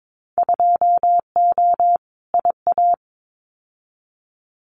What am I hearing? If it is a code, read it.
Morse code, heard as a keyed tone, about 22 wpm, 700 Hz. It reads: 2O IA
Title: Winter Field Day Exchanges 22wpm